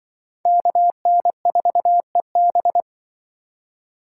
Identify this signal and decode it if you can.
Morse code, heard as a keyed tone, about 25 wpm, 700 Hz. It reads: KN4EB
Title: Speed Racing - USA 2x2 Call Signs 20wpm